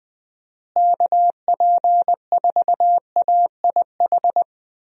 Morse code, heard as a keyed tone, about 20 wpm, 700 Hz. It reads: KP4AIH
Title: DXCC-100 Call Signs - 6 Characters - Repeated 2x - 20wpm